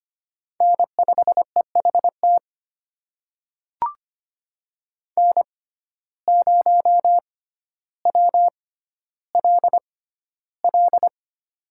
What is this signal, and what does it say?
Morse code, heard as a keyed tone, about 25 wpm, spaced as for 13 wpm, 700 Hz. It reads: N5EHT N0WLL
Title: USA Call Signs - 5 Characters - 6x Character Spacing - 25wpm